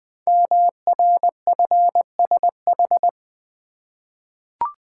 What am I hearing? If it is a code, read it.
Morse code, heard as a keyed tone, about 20 wpm, 700 Hz. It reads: MRFSH
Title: Code Groups - 5 Letters - 20wpm